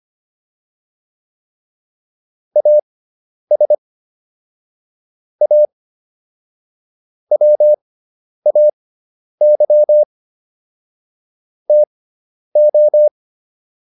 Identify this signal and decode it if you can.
Morse code, heard as a keyed tone, about 25 wpm, spaced as for 10 wpm, 600 Hz. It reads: AS A WAY TO